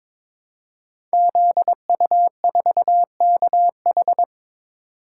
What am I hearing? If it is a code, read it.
Morse code, heard as a keyed tone, about 22 wpm, 700 Hz. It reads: ZU4KH
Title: Code Groups - 5 Letter:Numbers - 22wpm